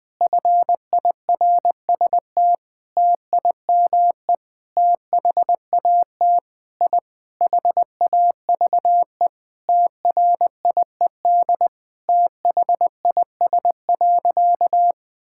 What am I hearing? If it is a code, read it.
Morse code, heard as a keyed tone, about 20 wpm, 700 Hz. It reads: FIRST TIME THAT I HAVE TRIED THIS.
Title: Sentences with 6-Word Phrases - Spoken Beginning - 20wpm